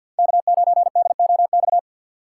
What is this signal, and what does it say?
Morse code, heard as a keyed tone, about 47 wpm, 690 Hz. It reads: K9DOX
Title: USA Call Signs - 2x Character Spacing - 50wpm